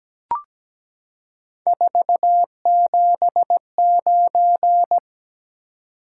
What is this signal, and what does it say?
Morse code, heard as a keyed tone, about 17 wpm, 700 Hz. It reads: 479
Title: Top 180 QSO Elements 17wpm